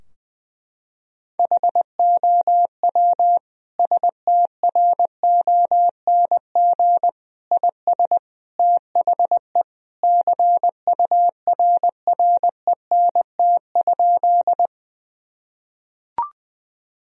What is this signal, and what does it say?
Morse code, heard as a keyed tone, about 19 wpm, 700 Hz. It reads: HOW STRONG IS THE CURRENT?
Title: Sentences from Top 1000 Words - Review Part 4 - 20wpm